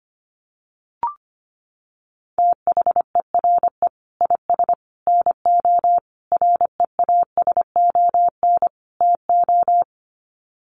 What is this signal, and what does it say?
Morse code, heard as a keyed tone, about 25 wpm, 700 Hz. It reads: THERE IS NO REASON TO
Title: Sets of 5 Words Encore 25wpm